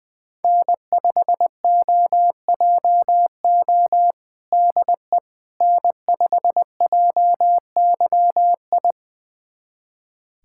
Morse code, heard as a keyed tone, about 21 wpm, 700 Hz. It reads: N5OJO DE N5JYI
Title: USA Callsign DE Callsign - 5 Characters - 20wpm